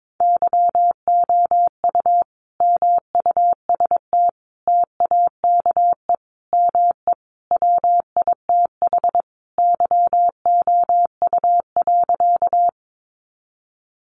Morse code, heard as a keyed tone, about 22 wpm, 700 Hz. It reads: YOU MUST TAKE ME WITH YOU.
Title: Sentences from Top 200 Words 22wpm